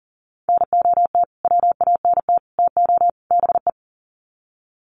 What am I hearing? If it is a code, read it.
Morse code, heard as a keyed tone, about 38 wpm, 700 Hz. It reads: NOT WANT TO BE
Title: Sets of 4 Words 40wpm